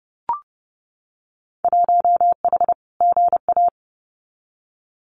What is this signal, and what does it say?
Morse code, heard as a keyed tone, about 31 wpm, 700 Hz. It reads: 1H GA
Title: Winter Field Day Exchanges 30wpm